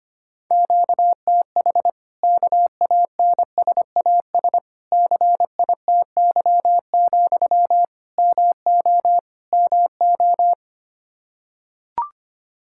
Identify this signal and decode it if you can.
Morse code, heard as a keyed tone, about 25 wpm, 700 Hz. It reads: QTH KANSAS CITY, MO MO
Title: Practice QSO - Exchange 1 - Beginning - 25wpm